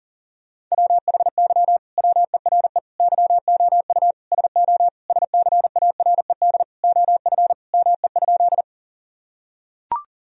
Morse code, heard as a keyed tone, about 37 wpm, 700 Hz. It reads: WHY WERE YOU SO SCARED OF ME?
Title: Sentences from Top 1000 Words - Review Part 4 - 40wpm